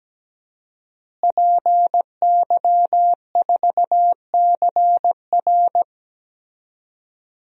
Morse code, heard as a keyed tone, about 17 wpm, 700 Hz. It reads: PY4CR
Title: DXCC-100 Call Signs - 5 Characters - Repeated 2x - 17wpm